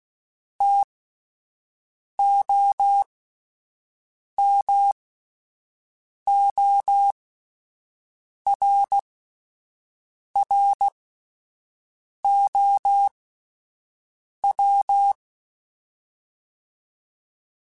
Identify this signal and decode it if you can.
Morse code, heard as a keyed tone, about 16 wpm, spaced as for 5 wpm, 800 Hz. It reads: TOMORROW
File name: Spaced / Chuckles Spaced-mp3s / SoftWords sp @10